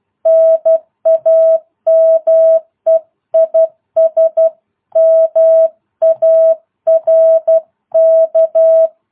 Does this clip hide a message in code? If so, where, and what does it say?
Yes: Morse code from 0.3 to 8.9 s, heard as a keyed tone, about 12 wpm, 650 Hz. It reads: NAME IS MARK